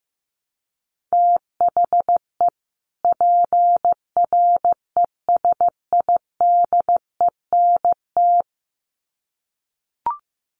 Morse code, heard as a keyed tone, about 15 wpm, 700 Hz. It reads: THE PRESIDENT
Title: Sets of 2 Words 15wpm